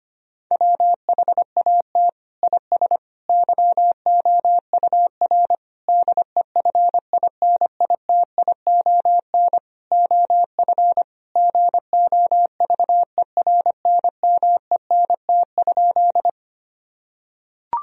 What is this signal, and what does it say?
Morse code, heard as a keyed tone, about 25 wpm, 700 Hz. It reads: WHAT IS YOUR DEFINITION OF GOVERNMENT?
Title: Sentences from Top 1000 Words - Review Part 4 - 25wpm